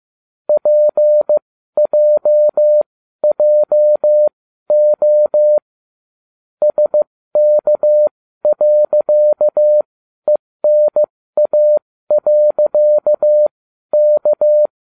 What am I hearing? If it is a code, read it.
Morse code, heard as a keyed tone, about 15 wpm, spaced as for 12 wpm, 600 Hz. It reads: PJJO SK.ENA.K